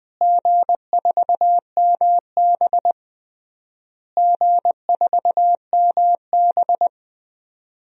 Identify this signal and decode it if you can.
Morse code, heard as a keyed tone, about 19 wpm, 700 Hz. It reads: G4MB G4MB
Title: DXCC-100 Call Signs - 4 Characters - Repeated 3x - 20wpm